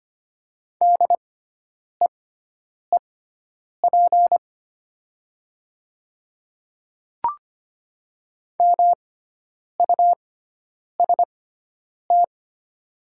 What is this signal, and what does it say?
Morse code, heard as a keyed tone, about 25 wpm, spaced as for 9 wpm, 700 Hz. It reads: DEEP MUST
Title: 6x Character Spacing - Top 500 Words - 25wpm